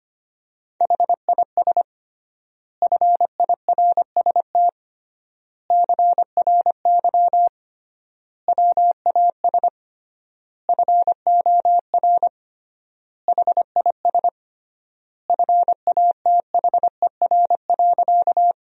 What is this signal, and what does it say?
Morse code, heard as a keyed tone, about 25 wpm, 700 Hz. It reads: HIS FIRST CRY WAS FOR HIS FATHER.
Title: 3x Word-Spacing - Sentences from Top 500 Words - 25wpm